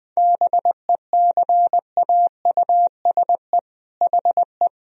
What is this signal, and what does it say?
Morse code, heard as a keyed tone, about 21 wpm, 700 Hz. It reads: BECAUSE HE
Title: Sentences from Top 1000 Words - Review Part 3 - 20wpm